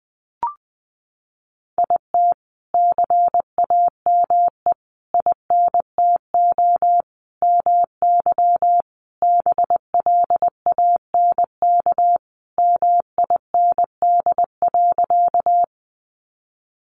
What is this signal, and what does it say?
Morse code, heard as a keyed tone, about 19 wpm, 700 Hz. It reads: IT CAME INTO MY BLANK MIND.
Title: Sentences from Top 1500 Words 20wpm